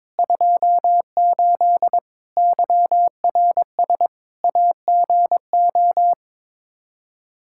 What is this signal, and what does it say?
Morse code, heard as a keyed tone, about 22 wpm, 700 Hz. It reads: 28 YRS AGO
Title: K1USN SST QSO Practice 22wpm